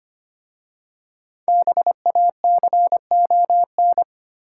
Morse code, heard as a keyed tone, about 25 wpm, 700 Hz. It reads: BACON
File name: practice / 25wpm / Difficult Words with Hints - 25wpm